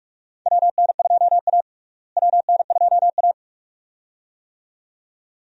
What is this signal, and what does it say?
Morse code, heard as a keyed tone, about 48 wpm, 700 Hz. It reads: WN2A WN2A
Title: USA Call Signs - 4 Characters - Repeated 2x - 45wpm